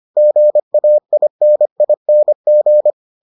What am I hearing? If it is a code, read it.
Morse code, heard as a keyed tone, about 25 wpm, 600 Hz. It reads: GAINING